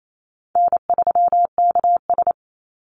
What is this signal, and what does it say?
Morse code, heard as a keyed tone, about 28 wpm, 700 Hz. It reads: N3KS
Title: Easy Call Signs - 28wpm